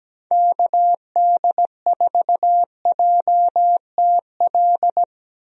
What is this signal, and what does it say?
Morse code, heard as a keyed tone, about 17 wpm, 700 Hz. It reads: KD4JTL
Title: USA Call Signs - 6 Characters - Repeated 2x - 17wpm